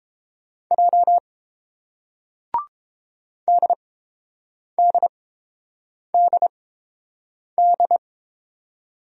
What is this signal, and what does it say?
Morse code, heard as a keyed tone, about 30 wpm, spaced as for 7 wpm, 700 Hz. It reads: J DDDD